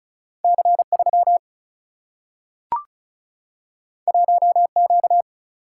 Morse code, heard as a keyed tone, about 36 wpm, 700 Hz. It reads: C3 1Q
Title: Single-Letter Single-Number 35wpm